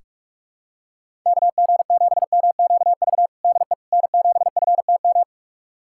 Morse code, heard as a keyed tone, about 46 wpm, 700 Hz. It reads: KG7MQV DE N7FTM
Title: USA Callsign DE Callsign 45wpm